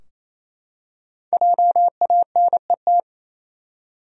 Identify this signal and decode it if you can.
Morse code, heard as a keyed tone, about 27 wpm, 700 Hz. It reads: JANET
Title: USA Common Names - 28wpm